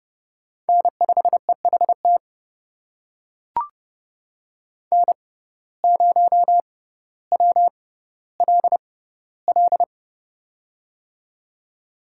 Morse code, heard as a keyed tone, about 29 wpm, spaced as for 15 wpm, 700 Hz. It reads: N5EHT N0WLL
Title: USA Call Signs - 5 Characters - 6x Character Spacing - 30wpm